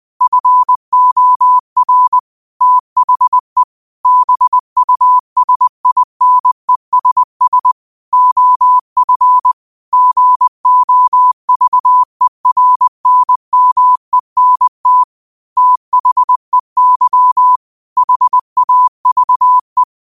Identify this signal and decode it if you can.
Morse code, heard as a keyed tone, about 20 wpm, 1000 Hz. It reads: FOR THE BUSINESS OF GOVERNMENT THEY HAVE
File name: qotdfiles / 20wpm